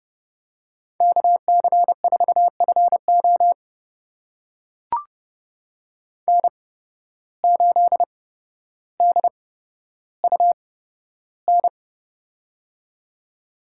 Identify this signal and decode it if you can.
Morse code, heard as a keyed tone, about 30 wpm, spaced as for 13 wpm, 700 Hz. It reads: KC4FO N8DUN
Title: USA Call Signs - 5 Characters - 8x Character Spacing - 30wpm